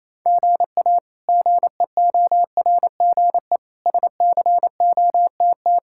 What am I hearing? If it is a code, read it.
Morse code, heard as a keyed tone, about 27 wpm, 700 Hz. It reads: GA GEORGE SCOTT
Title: K1USN SST QSO Practice - 28wpm